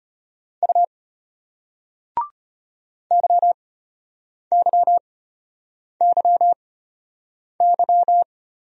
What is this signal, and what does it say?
Morse code, heard as a keyed tone, about 34 wpm, spaced as for 8 wpm, 700 Hz. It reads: U YYYY